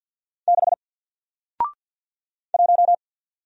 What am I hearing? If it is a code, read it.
Morse code, heard as a keyed tone, about 48 wpm, 700 Hz. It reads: 6 1
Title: Single Character (Letter:Number:Punctuation:Prosign) Alt 50wpm